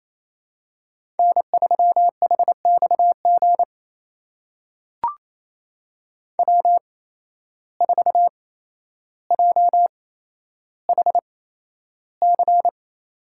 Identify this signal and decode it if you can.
Morse code, heard as a keyed tone, about 28 wpm, spaced as for 12 wpm, 700 Hz. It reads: N3HXG W4JHC
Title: USA Call Signs - 5 Characters - 8x Character Spacing - 28wpm